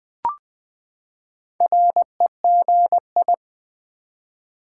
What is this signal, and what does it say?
Morse code, heard as a keyed tone, about 20 wpm, 700 Hz. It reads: REGI
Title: Build A Word - 4 Letters At A Time - 12-Letter Words - 20wpm